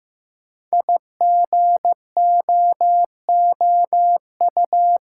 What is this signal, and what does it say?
Morse code, heard as a keyed tone, about 15 wpm, 700 Hz. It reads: IGOOU
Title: Code Groups - 5 Letters - 15wpm